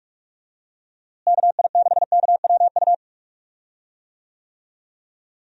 Morse code, heard as a keyed tone, about 46 wpm, 700 Hz. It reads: KI6KWU
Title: USA Call Signs - 6 Characters - Repeated 2x - 45wpm